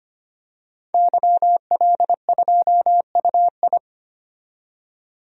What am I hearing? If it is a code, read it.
Morse code, heard as a keyed tone, about 25 wpm, 700 Hz. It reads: YL2UI
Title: DXCC-100 Call Signs - 5 Characters - 25wpm